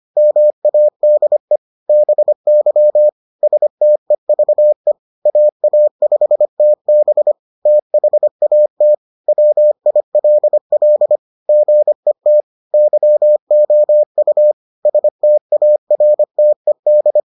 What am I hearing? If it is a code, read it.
Morse code, heard as a keyed tone, about 25 wpm, 600 Hz. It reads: MADE BY STEVE AA5TB THAT WILL GET YOU STARTED